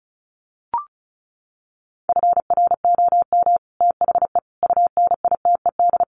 Morse code, heard as a keyed tone, about 34 wpm, 700 Hz. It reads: FROM THE UNITED
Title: Sets of 4 Words 35wpm